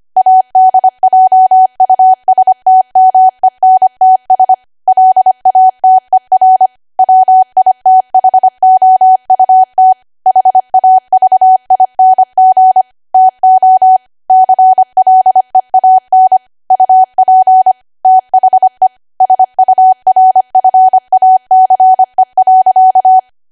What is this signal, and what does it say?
Morse code, heard as a keyed tone, about 25 wpm, 750 Hz. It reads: ADJUSTMENTS LATER WITHOUT HAVING TO CLEAN UP THE SURFACE.
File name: Archive / 25 WPM / 25WPM